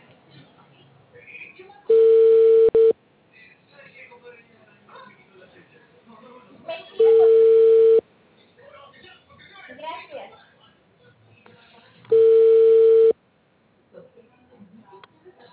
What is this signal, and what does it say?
Ringback tone (China, 450 Hz), with speech.